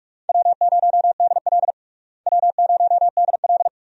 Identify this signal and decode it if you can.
Morse code, heard as a keyed tone, about 46 wpm, 700 Hz. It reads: W0DL W0DL
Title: USA Call Signs - 4 Characters - Repeated 2x - 45wpm